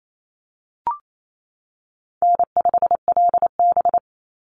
Morse code, heard as a keyed tone, about 28 wpm, 700 Hz. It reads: N5LB
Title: Easy Call Signs - 28wpm